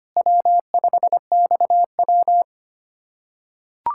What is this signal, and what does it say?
Morse code, heard as a keyed tone, about 25 wpm, 700 Hz. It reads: W5XW
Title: USA Callsign DE Callsign - 4 Characters - 25wpm